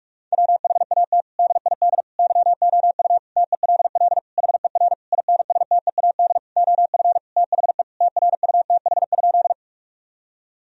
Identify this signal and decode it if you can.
Morse code, heard as a keyed tone, about 46 wpm, 700 Hz. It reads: WHAT DID YOU TELL HER INSTEAD OF THE TRUTH?
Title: Sentences from Top 1000 Words - Review Part 4 - 45wpm